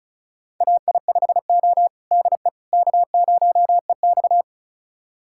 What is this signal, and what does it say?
Morse code, heard as a keyed tone, about 34 wpm, 700 Hz. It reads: AI5O DE K0EX
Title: USA Callsign DE Callsign - 4 Characters - 35wpm